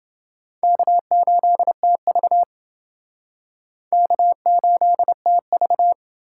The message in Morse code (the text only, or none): K8TV K8TV